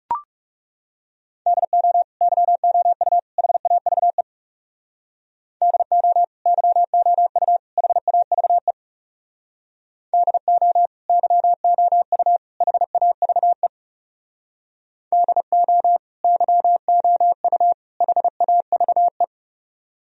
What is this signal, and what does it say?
Morse code, heard as a keyed tone, about 40 wpm, 700 Hz. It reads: DO YOU HAVE DO YOU HAVE DO YOU HAVE DO YOU HAVE